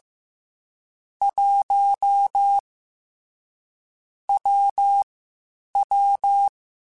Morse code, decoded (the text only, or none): 1 W W